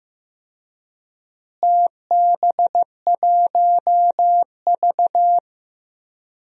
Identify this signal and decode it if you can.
Morse code, heard as a keyed tone, about 15 wpm, 700 Hz. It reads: TB1V
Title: DXCC-100 Call Signs - 4 Characters - Repeated 2x - 15wpm